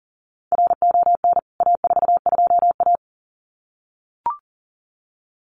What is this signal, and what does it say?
Morse code, heard as a keyed tone, about 37 wpm, 700 Hz. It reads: RON A42A
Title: CWT Exchange 40wpm